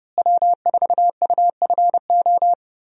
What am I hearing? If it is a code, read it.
Morse code, heard as a keyed tone, about 30 wpm, 700 Hz. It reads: W4UFO
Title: USA Call Signs - 5 Characters - Repeated 2x - 30wpm